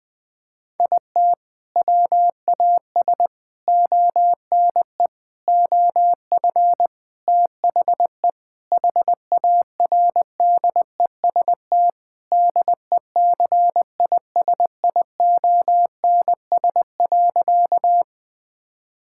Morse code, heard as a keyed tone, about 21 wpm, 700 Hz. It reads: IT WAS ONE OF THE HARDEST DECISIONS.
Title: Sentences with 5-Word Phrases - 20wpm